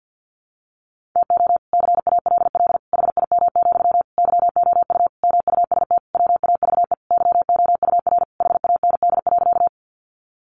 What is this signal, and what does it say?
Morse code, heard as a keyed tone, about 50 wpm, 700 Hz. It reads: TO CALL HIM, YOU MUST WAVE YOUR HAND.
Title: Sentences from Top 1400 Words 50wpm